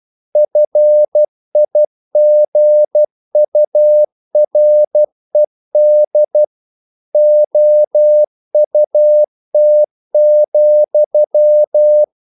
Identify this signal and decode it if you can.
Morse code, heard as a keyed tone, about 12 wpm, 600 Hz. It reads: FIGURED OUT,
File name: CW / Practice / EnglishQuotations0312